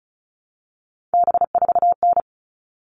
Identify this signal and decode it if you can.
Morse code, heard as a keyed tone, about 34 wpm, 700 Hz. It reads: B4N
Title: All Cut Numbers - 3-Digit 35wpm